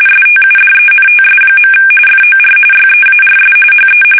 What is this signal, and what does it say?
RTTY, about 45.45 baud. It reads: STUDENTS. FOR YEARS HE
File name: fsk850 / benjamin-g-lamme